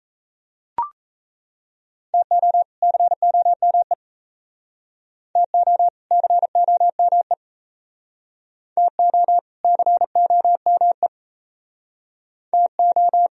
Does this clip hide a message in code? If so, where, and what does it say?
Yes: Morse code from 2.1 to 13.3 s, heard as a keyed tone, about 35 wpm, 700 Hz. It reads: TO COME TO COME TO COME TO